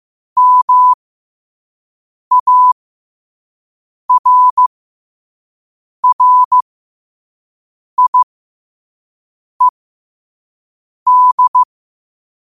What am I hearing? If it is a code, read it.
Morse code, heard as a keyed tone, about 15 wpm, spaced as for 5 wpm, 1000 Hz. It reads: MARRIED
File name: qotdfiles / 5wpm